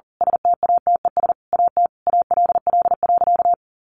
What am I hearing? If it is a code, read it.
Morse code, heard as a keyed tone, about 40 wpm, 700 Hz. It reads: STATES AT ALL.
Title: A Sound Of Thunder 40wpm